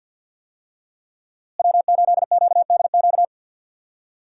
Morse code, heard as a keyed tone, about 51 wpm, 700 Hz. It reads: W8QDQ
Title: USA Call Signs - 5 Characters - Repeated 2x - 50wpm